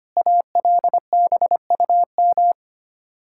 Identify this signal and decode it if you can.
Morse code, heard as a keyed tone, about 25 wpm, 700 Hz. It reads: ALBUM
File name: practice / 25wpm / Difficult Words with Hints - 25wpm